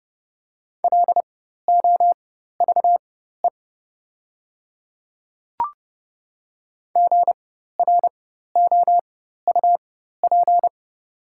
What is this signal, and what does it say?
Morse code, heard as a keyed tone, about 31 wpm, spaced as for 14 wpm, 700 Hz. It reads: LOVE GROUP
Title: 4x Character Spacing - Top 500 Words - 30wpm